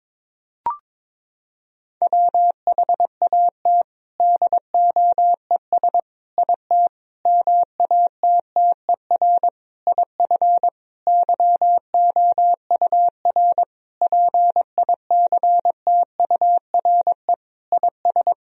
Morse code, heard as a keyed tone, about 22 wpm, 700 Hz. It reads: WHAT DOES IT MATTER IF YOUR PICTURE IS
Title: Sentences from Top 1000 Words - Review Part 1 - 22wpm